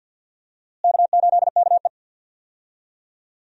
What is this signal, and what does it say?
Morse code, heard as a keyed tone, about 53 wpm, 700 Hz. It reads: K8KE
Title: USA Call Signs - 4 Characters - Repeated 3x - 50wpm